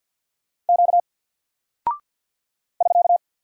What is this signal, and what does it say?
Morse code, heard as a keyed tone, about 48 wpm, 700 Hz. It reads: Q <SK>
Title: Single Character (Letter:Number:Punctuation:Prosign) 50wpm